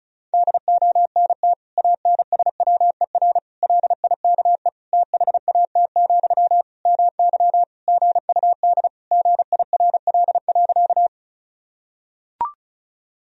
Morse code, heard as a keyed tone, about 36 wpm, 700 Hz. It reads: DONT ANSWER LIKE THAT, MY GUD GIRL.
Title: Sentences from Top 400 Words and Abbreviations 35wpm